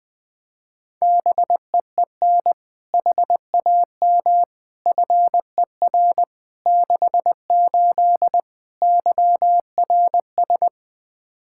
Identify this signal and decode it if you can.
Morse code, heard as a keyed tone, about 20 wpm, 700 Hz. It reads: BEEN HAM FER 68 YRS
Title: Practice QSO - Exchange 3 - More Info - 20wpm